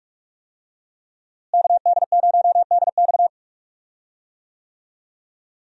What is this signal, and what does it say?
Morse code, heard as a keyed tone, about 42 wpm, 700 Hz. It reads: KD0DX
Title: USA Call Signs - 5 Characters - Repeated 2x - 45wpm